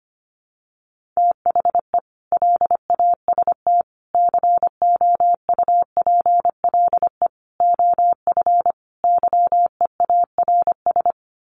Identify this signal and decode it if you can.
Morse code, heard as a keyed tone, about 25 wpm, 700 Hz. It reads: THE LAST COUPLE OF YEARS
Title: Sets of 5 Words 25wpm